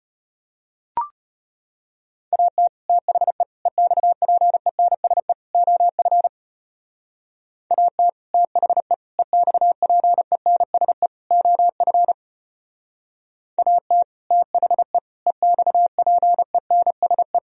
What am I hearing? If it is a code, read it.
Morse code, heard as a keyed tone, about 32 wpm, 700 Hz. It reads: AT THE EXPENSE OF AT THE EXPENSE OF AT THE EXPENSE